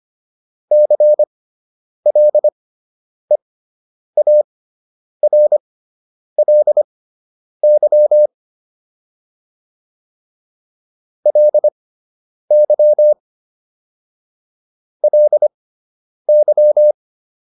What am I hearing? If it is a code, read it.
Morse code, heard as a keyed tone, about 25 wpm, spaced as for 9 wpm, 600 Hz. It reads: CLEARLY LY LY
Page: fundamental-practice-files